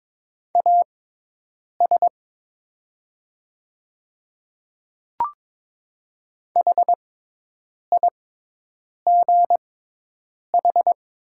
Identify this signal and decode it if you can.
Morse code, heard as a keyed tone, about 22 wpm, spaced as for 8 wpm, 700 Hz. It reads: AS HIGH
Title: 6x Character Spacing - Top 200 Words - 22wpm